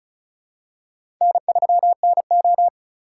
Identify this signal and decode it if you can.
Morse code, heard as a keyed tone, about 36 wpm, 700 Hz. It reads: N3NO
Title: USA Call Signs - 4 Characters - 3x Character Spacing - 35wpm